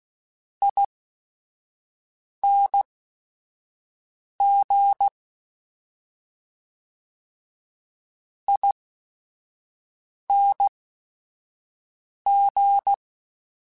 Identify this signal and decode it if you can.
Morse code, heard as a keyed tone, about 16 wpm, spaced as for 5 wpm, 800 Hz. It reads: ING ING
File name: Special INGs / Special INGs-mp3s / SlowToFaster ing @5